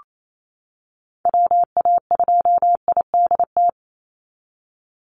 Morse code, heard as a keyed tone, about 28 wpm, 700 Hz. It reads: WA2IDT
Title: USA Call Signs 28wpm